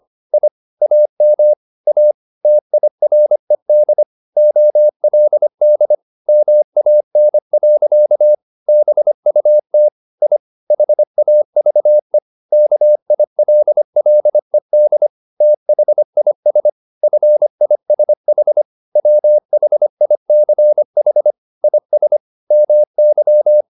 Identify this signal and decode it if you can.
Morse code, heard as a keyed tone, about 25 wpm, 600 Hz. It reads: I AM A TIRED OLD MAN. BUT I HAVE KILLED THIS FISH WHICH IS MY